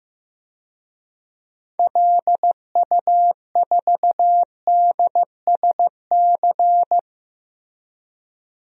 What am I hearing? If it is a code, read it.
Morse code, heard as a keyed tone, about 15 wpm, 700 Hz. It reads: LU4DSC
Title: DXCC-100 Call Signs - 6 Characters - 15wpm